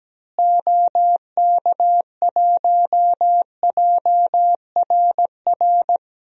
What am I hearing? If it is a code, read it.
Morse code, heard as a keyed tone, about 17 wpm, 700 Hz. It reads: OK1JRR
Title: DXCC-100 Call Signs - 6 Characters - Repeated 3x - 17wpm